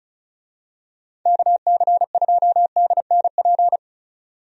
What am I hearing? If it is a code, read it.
Morse code, heard as a keyed tone, about 34 wpm, 700 Hz. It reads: KC2DNP
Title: USA Call Signs - 6 Characters - 35wpm